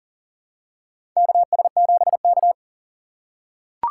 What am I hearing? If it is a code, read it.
Morse code, heard as a keyed tone, about 40 wpm, 700 Hz. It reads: KS7K
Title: USA Call Signs - 4 Characters - 3x Character Spacing - 40wpm